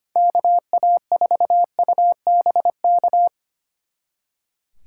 Morse code, heard as a keyed tone, about 28 wpm, 700 Hz. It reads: KA4UBK
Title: Speed Racing - USA 2x3 Call Signs 25wpm